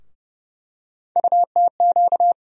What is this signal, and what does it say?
Morse code, heard as a keyed tone, about 30 wpm, 700 Hz. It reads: UTQ
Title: USA Call Sign Postfixes - 3 Letters 30wpm